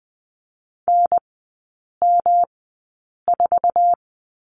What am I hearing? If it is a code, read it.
Morse code, heard as a keyed tone, about 20 wpm, spaced as for 8 wpm, 700 Hz. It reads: NM4
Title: 107 Lesson 18e - Letter m - Three Characters 20wpm